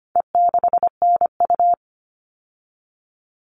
Morse code, heard as a keyed tone, about 25 wpm, 700 Hz. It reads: E6NU
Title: All Cut Numbers - 4-Digit 25wpm